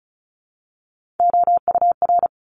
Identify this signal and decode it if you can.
Morse code, heard as a keyed tone, about 34 wpm, 700 Hz. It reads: OUR
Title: Top 200 Words Repeated 35wpm